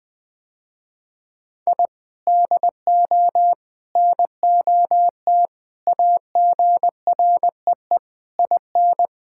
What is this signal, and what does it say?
Morse code, heard as a keyed tone, about 21 wpm, 700 Hz. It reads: I DO NOT AGREE IN
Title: Sentences with 6-Word Phrases - Spoken Beginning - 20wpm